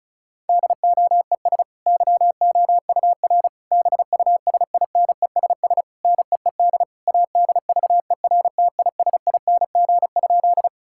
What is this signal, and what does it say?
Morse code, heard as a keyed tone, about 36 wpm, 700 Hz. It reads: DOES YOUR BUSINESS NEED ADVERTISING?
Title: Sentences from Top 1000 Words - Review Part 6 - 35wpm